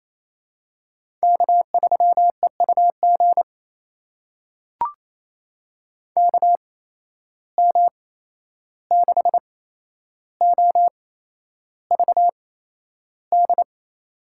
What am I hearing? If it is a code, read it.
Morse code, heard as a keyed tone, about 28 wpm, spaced as for 11 wpm, 700 Hz. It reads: K3EUG KM6OVD
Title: USA Call Signs - 8x Character Spacing - 28wpm